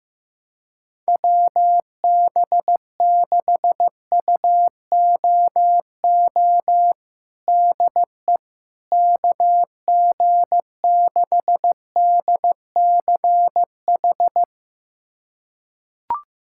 Morse code, heard as a keyed tone, about 15 wpm, 700 Hz. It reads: WB6UOO DE KG6DCH
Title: USA Callsign DE Callsign 15wpm